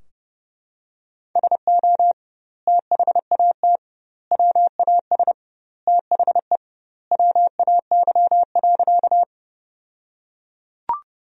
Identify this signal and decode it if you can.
Morse code, heard as a keyed tone, about 30 wpm, 700 Hz. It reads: SO THAT WAS THE WAY.
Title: 2x Word-Spacing - Sentences from Top 100 Words - 30wpm